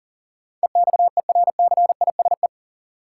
Morse code, heard as a keyed tone, about 40 wpm, 700 Hz. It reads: EXERCISE
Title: Sentences from Top 1000 Words - Review Part 4 - New Word Drill - 40wpm